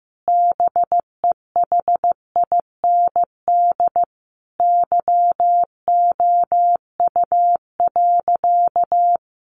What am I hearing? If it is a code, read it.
Morse code, heard as a keyed tone, about 15 wpm, 700 Hz. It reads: BEHIND YOU.
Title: Sentences from Top 500 Words 15wpm